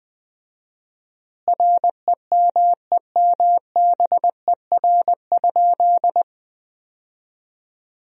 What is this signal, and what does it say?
Morse code, heard as a keyed tone, about 20 wpm, 700 Hz. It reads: REMEMBER?
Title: 4x Word-Spacing - Sentences from Top 400 Words - 20wpm